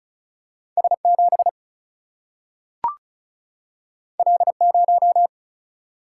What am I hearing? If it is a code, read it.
Morse code, heard as a keyed tone, about 36 wpm, 700 Hz. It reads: S 7 L 0
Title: Single-Letter Single-Number 35wpm